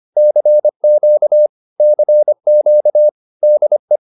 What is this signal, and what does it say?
Morse code, heard as a keyed tone, about 25 wpm, 600 Hz. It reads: CQ CQ DE